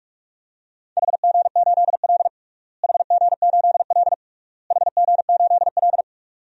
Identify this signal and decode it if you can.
Morse code, heard as a keyed tone, about 41 wpm, 700 Hz. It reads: HG8L HG8L HG8L
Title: DXCC-100 Call Signs - 4 Characters - Repeated 3x - 45wpm